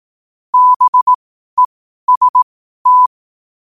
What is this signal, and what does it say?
Morse code, heard as a keyed tone, about 18 wpm, spaced as for 12 wpm, 1000 Hz. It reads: BEST
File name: qotdfiles / 12wpm